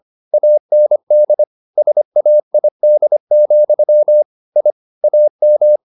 Morse code, heard as a keyed tone, about 25 wpm, 600 Hz. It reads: AND SAID, I AM
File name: the old man and the sea part 1 25wpm